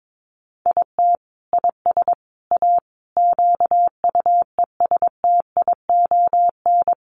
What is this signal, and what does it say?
Morse code, heard as a keyed tone, about 22 wpm, 700 Hz. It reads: IT IS A QUESTION
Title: Sentences from Top 1100 Words 22wpm